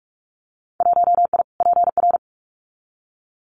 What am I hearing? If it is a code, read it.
Morse code, heard as a keyed tone, about 44 wpm, 700 Hz. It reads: 1I PR
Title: Winter Field Day Exchanges Encore 45wpm